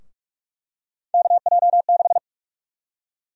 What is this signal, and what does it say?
Morse code, heard as a keyed tone, about 45 wpm, 700 Hz. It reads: KJ6
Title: USA Call Sign Prefixes - Repeated 2x - 45wpm